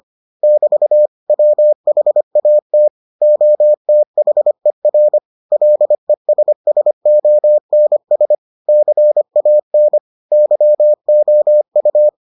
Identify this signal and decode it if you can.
Morse code, heard as a keyed tone, about 25 wpm, 600 Hz. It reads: <BT> WHAT OTHER LESSONS CAN YOU